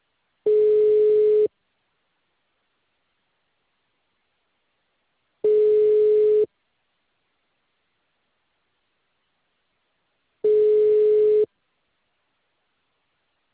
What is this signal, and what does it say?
Ringback tone (Europe, 425 Hz).
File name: out- -2021- - - .78946